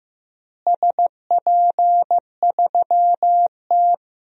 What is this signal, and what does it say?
Morse code, heard as a keyed tone, about 15 wpm, 700 Hz. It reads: SP3T
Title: DXCC-100 Call Signs - 4 Characters - Repeated 3x - 15wpm